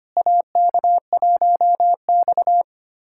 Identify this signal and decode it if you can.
Morse code, heard as a keyed tone, about 25 wpm, 700 Hz. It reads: AK1X
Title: USA Callsign DE Callsign - 4 Characters - 25wpm